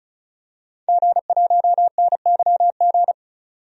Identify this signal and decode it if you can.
Morse code, heard as a keyed tone, about 34 wpm, 700 Hz. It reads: G1NYG
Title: DXCC-100 Call Signs - 5 Characters - 35wpm